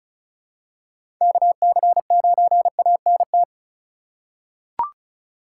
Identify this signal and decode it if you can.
Morse code, heard as a keyed tone, about 36 wpm, 700 Hz. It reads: KC9ANT
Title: USA Call Signs - 6 Characters - 4x Character Spacing - 35wpm